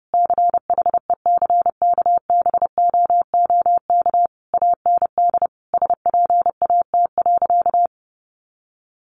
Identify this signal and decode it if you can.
Morse code, heard as a keyed tone, about 30 wpm, 700 Hz. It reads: CHECKBOOK AND SPAT.
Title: A Sound Of Thunder 30wpm